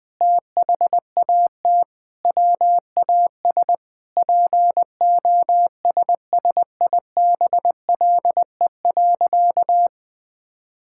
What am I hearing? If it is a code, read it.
Morse code, heard as a keyed tone, about 20 wpm, 700 Hz. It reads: THAT WAS POSSIBLE.
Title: Sentences with 5-Word Phrases - Spoken Beginning - 20wpm